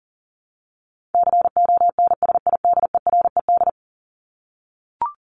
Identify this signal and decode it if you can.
Morse code, heard as a keyed tone, about 40 wpm, 700 Hz. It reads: CONSIDERED
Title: ED Suffix - Top 100 Words with Base Words 40wpm